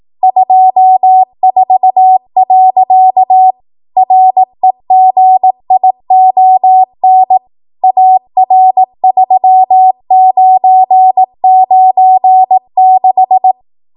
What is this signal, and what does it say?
Morse code, heard as a keyed tone, about 18 wpm, 750 Hz. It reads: 24. REGION AR3996